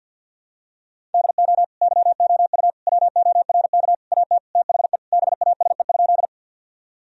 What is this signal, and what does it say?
Morse code, heard as a keyed tone, about 52 wpm, 700 Hz. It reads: DO YOU WORK AT THE BASE?
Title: Sentences from Top 1000 Words - Review Part 1 - 50wpm